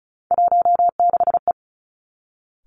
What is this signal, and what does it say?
Morse code, heard as a keyed tone, about 36 wpm, 700 Hz. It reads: 16E
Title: Three Challenging Letter-Number Combinations 35wpm